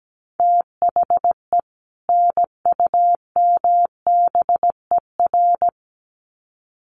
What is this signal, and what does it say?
Morse code, heard as a keyed tone, about 17 wpm, 700 Hz. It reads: THE NUMBER
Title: Sets of 2 Words 17wpm